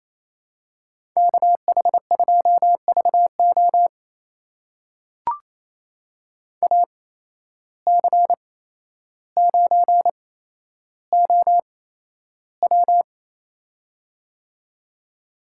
Morse code, heard as a keyed tone, about 28 wpm, spaced as for 12 wpm, 700 Hz. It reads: KH2VO AC9OW
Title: USA Call Signs - 5 Characters - 8x Character Spacing - 28wpm